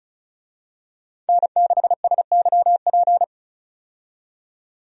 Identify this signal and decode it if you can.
Morse code, heard as a keyed tone, about 34 wpm, 700 Hz. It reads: N6SYP
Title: USA Call Signs - 5 Characters - 35wpm